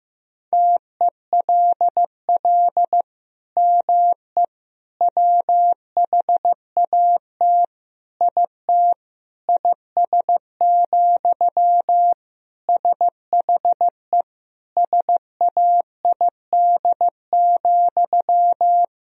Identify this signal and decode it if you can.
Morse code, heard as a keyed tone, about 16 wpm, 700 Hz. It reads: TELL ME WHAT IT IS, SHE SAID,
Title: Sentences from Top 1000 Words - Review Part 1 - 15wpm